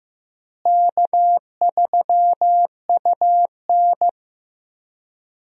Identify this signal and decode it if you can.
Morse code, heard as a keyed tone, about 15 wpm, 700 Hz. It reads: K3UN
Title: USA Call Signs - 4 Characters - Repeated 2x - 15wpm